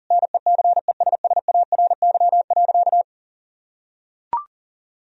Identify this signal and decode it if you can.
Morse code, heard as a keyed tone, about 40 wpm, 700 Hz. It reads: NECESSARY.
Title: Sentences from Top 1000 Words - Review Part 4 - 40wpm